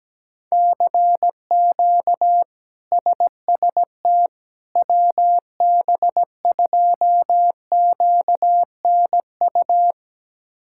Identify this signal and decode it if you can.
Morse code, heard as a keyed tone, about 17 wpm, 700 Hz. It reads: CQ SST WB2QNU
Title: K1USN SST QSO Practice - 17wpm